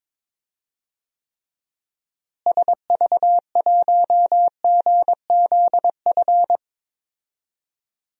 Morse code, heard as a keyed tone, about 22 wpm, 700 Hz. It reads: SV1GZF
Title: DXCC-100 Call Signs - 6 Characters - Repeated 3x - 22wpm